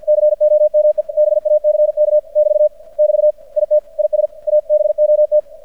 Morse code, heard as a keyed tone, about 50 wpm, 600 Hz. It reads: COME BACK <BT> <BT> IT IS ABOT
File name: archived / CRSnet